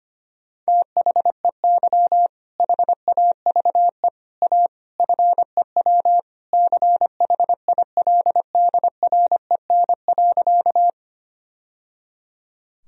Morse code, heard as a keyed tone, about 25 wpm, 700 Hz. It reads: THEY HAVE A FEW CHILDREN.
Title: Sentences from Top 1000 Words - Review Part 1 - 25wpm